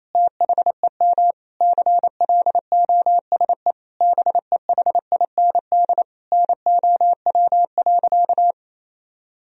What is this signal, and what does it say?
Morse code, heard as a keyed tone, about 28 wpm, 700 Hz. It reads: THEM CLOSE BEHIND NOW.
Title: Sentences from Top 1000 Words - Review Part 1 - 28wpm